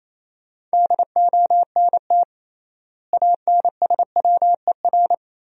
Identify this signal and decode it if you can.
Morse code, heard as a keyed tone, about 28 wpm, 700 Hz. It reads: DONT ANSWER
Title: 3x Word-Spacing - Sentences from Top 400 Words - 28wpm